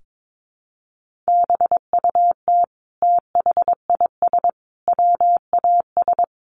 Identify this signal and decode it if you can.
Morse code, heard as a keyed tone, about 22 wpm, 700 Hz. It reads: BUT THIS WAS
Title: Sentences from Top 1400 Words 22wpm